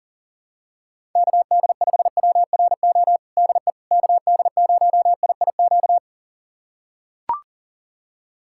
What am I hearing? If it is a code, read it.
Morse code, heard as a keyed tone, about 40 wpm, 700 Hz. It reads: KD5WRO DE KD0IIQ
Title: USA Callsign DE Callsign - 6 Characters - 40wpm